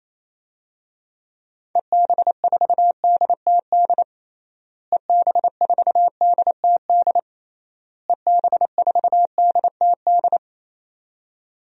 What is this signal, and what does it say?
Morse code, heard as a keyed tone, about 28 wpm, 700 Hz. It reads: EB4DTD EB4DTD EB4DTD
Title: DXCC-100 Call Signs - 6 Characters - Repeated 3x - 28wpm